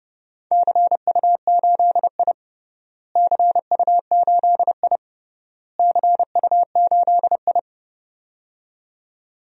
Morse code, heard as a keyed tone, about 31 wpm, 700 Hz. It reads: CU8I CU8I CU8I
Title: DXCC-100 Call Signs - 4 Characters - Repeated 3x - 30wpm